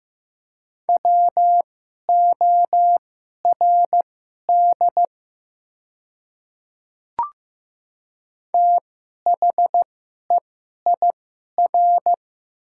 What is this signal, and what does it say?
Morse code, heard as a keyed tone, about 15 wpm, spaced as for 11 wpm, 700 Hz. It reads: WORD THEIR
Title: 2x Character Spacing - Top 100 Words - 15wpm